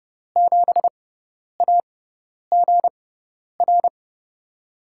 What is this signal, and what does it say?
Morse code, heard as a keyed tone, about 30 wpm, spaced as for 10 wpm, 700 Hz. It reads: 7AGR
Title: USA Call Signs - 5 Characters - 6x Character Spacing - 30wpm